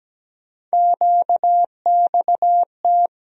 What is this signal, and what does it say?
Morse code, heard as a keyed tone, about 17 wpm, 700 Hz. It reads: QXT
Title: USA Call Sign Postfixes - 3 Letters 17wpm